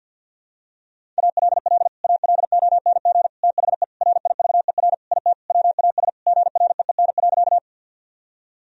Morse code, heard as a keyed tone, about 47 wpm, 700 Hz. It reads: ALL ALONG THE RIVER IT WAS GREEN.
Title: Sentences from Top 1000 Words - Review Part 2 - 50wpm